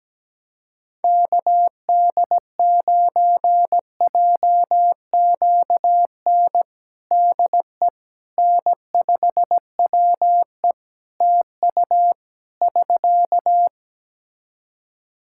Morse code, heard as a keyed tone, about 17 wpm, 700 Hz. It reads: KD9JQN DE N5WE TU <SK>
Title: Practice QSO - Exchange 4 - Ending - 17wpm